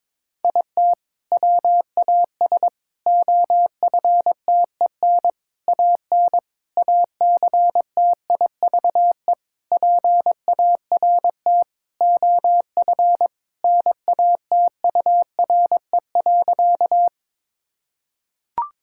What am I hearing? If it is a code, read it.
Morse code, heard as a keyed tone, about 21 wpm, 700 Hz. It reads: IT WAS OFTEN AN ACTIVE PART OF NATURE.
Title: Sentences from Top 1000 Words - Review Part 6 - 22wpm